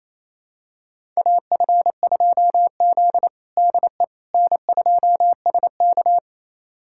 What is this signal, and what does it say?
Morse code, heard as a keyed tone, about 28 wpm, 700 Hz. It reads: AF2Z DE N2SK
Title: USA Callsign DE Callsign - 4 Characters - 28wpm